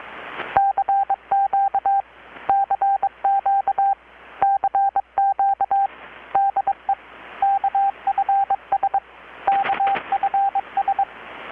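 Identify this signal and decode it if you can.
Morse code, heard as a keyed tone, about 22 wpm, 790 Hz, with radio noise. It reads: CQ CQ CQ DE KFS KFS
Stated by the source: Morse from KPH's Night of Nights XX maritime radiotelegraphy event, received in Chiapas, Mexico about 3,800 km away on a Kenwood R-600 with a grounded 10 m long-wire antenna.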